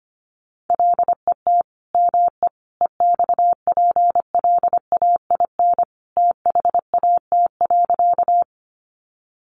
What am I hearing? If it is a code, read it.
Morse code, heard as a keyed tone, about 25 wpm, 700 Hz. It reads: LET ME EXPLAIN THAT.
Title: Sentences from Top 1100 Words 25wpm